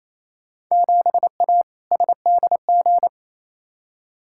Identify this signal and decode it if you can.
Morse code, heard as a keyed tone, about 28 wpm, 700 Hz. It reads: 7A SDG
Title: Field Day Exchanges Encore - 28wpm